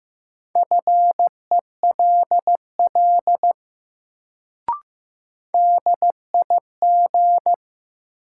Morse code, heard as a keyed tone, about 16 wpm, 700 Hz. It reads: FELL DIG
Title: Sentences from Top 1600 Words - New Word Drill - 15wpm